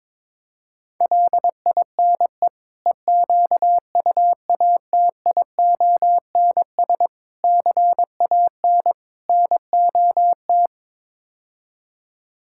Morse code, heard as a keyed tone, about 21 wpm, 700 Hz. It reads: LINE EQUATIONS CAN NOT
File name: practice / 22wpm / Build A Phrase - 5-Word Phrases - 22wpm